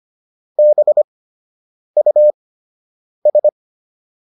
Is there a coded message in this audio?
Yes: Morse code from 0.6 to 3.5 s, heard as a keyed tone, about 25 wpm, spaced as for 8 wpm, 600 Hz. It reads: BUS